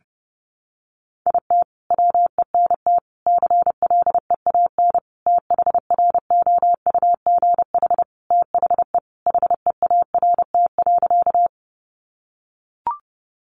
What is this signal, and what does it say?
Morse code, heard as a keyed tone, about 28 wpm, 700 Hz. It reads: IT WENT CLEAN THROUGH THE HEART.
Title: Sentences from Top 1200 Words 30wpm